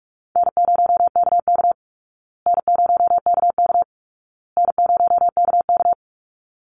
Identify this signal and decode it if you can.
Morse code, heard as a keyed tone, about 44 wpm, 700 Hz. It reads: N0KK N0KK N0KK
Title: Easy Call Signs Repeated 3x - 45wpm